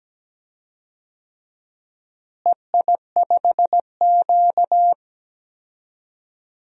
Morse code, heard as a keyed tone, about 17 wpm, 700 Hz. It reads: EI5Q
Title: DXCC-100 Call Signs - 4 Characters - Repeated 3x - 17wpm